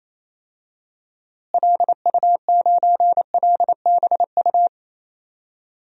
Morse code, heard as a keyed tone, about 28 wpm, 700 Hz. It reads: LU9LBU
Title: DXCC-100 Call Signs - 6 Characters - Repeated 2x - 28wpm